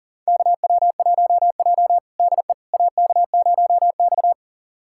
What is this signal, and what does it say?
Morse code, heard as a keyed tone, about 37 wpm, 700 Hz. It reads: KW1J DE AK0X
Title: USA Callsign DE Callsign - 4 Characters - 40wpm